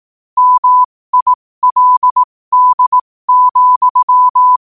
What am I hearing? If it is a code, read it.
Morse code, heard as a keyed tone, about 18 wpm, spaced as for 15 wpm, 1000 Hz. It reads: MILD,
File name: qotdfiles / 15wpm